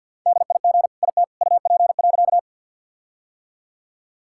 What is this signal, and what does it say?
Morse code, heard as a keyed tone, about 50 wpm, 700 Hz. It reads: DIG IT UP.
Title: Sentences from Top 1600 Words 50wpm